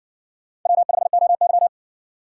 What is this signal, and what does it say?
Morse code, heard as a keyed tone, about 58 wpm, 700 Hz. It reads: W5OY